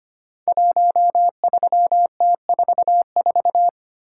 Morse code, heard as a keyed tone, about 25 wpm, 700 Hz. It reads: 13T44
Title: Code Groups - 5 Letter:Numbers - 25wpm